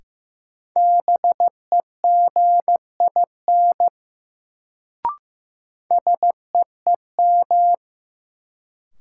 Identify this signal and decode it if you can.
Morse code, heard as a keyed tone, about 15 wpm, 700 Hz. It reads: BEGIN SEEM
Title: Sentences from Top 300 Words - New Word Drill - 15wpm